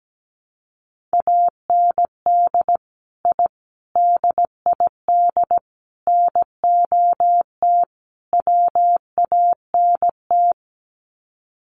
Morse code, heard as a keyed tone, about 17 wpm, 700 Hz. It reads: AND I DID NOT WANT
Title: Sets of 5 Words 17wpm